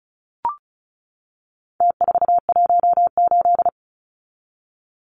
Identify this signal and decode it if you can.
Morse code, heard as a keyed tone, about 34 wpm, 700 Hz. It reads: T418
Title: Numbers - Contest Style - 4-Digit 35wpm